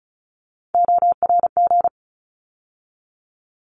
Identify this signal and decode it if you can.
Morse code, heard as a keyed tone, about 34 wpm, 700 Hz. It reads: ORG
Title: ARRL Sections 35wpm